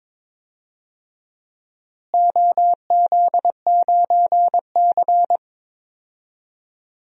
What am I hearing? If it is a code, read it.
Morse code, heard as a keyed tone, about 22 wpm, 700 Hz. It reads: OZ9C
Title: DXCC-100 Call Signs - 4 Characters - Repeated 2x - 22wpm